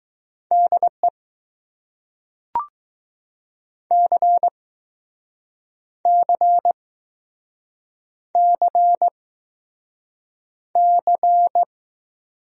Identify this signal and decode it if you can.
Morse code, heard as a keyed tone, about 20 wpm, spaced as for 6 wpm, 700 Hz. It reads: DE CCCC